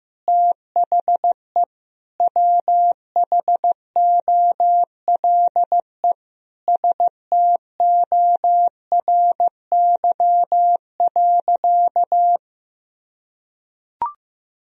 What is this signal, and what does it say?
Morse code, heard as a keyed tone, about 15 wpm, 700 Hz. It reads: THE WHOLE STORY.
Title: Sentences from Top 1000 Words - Review Part 3 - 15wpm